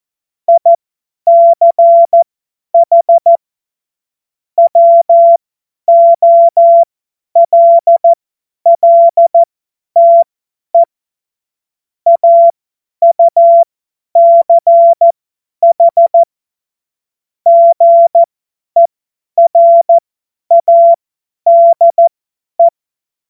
Morse code, heard as a keyed tone, about 14 wpm, spaced as for 10 wpm, 680 Hz. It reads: ICH WOLLTE AUCH GERADE
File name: Morsekurs / Witze / Witz 22. Februar 2022 50 F70